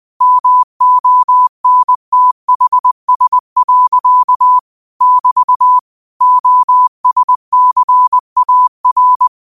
Morse code, heard as a keyed tone, about 20 wpm, 1000 Hz. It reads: MONTHS. <BT> OSCAR
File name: qotdfiles / 20wpm